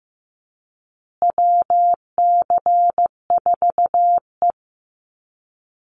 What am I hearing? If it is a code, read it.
Morse code, heard as a keyed tone, about 15 wpm, 700 Hz. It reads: WC4E
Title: Call Signs Repeated 2x - 15wpm